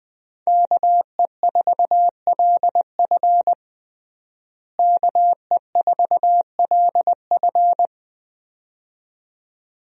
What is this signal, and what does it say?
Morse code, heard as a keyed tone, about 21 wpm, 700 Hz. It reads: KE4LF KE4LF
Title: USA Call Signs - 5 Characters - Repeated 2x - 20wpm